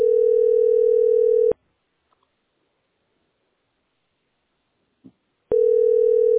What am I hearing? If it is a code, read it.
Ringback tone (North America), 440 + 480 Hz.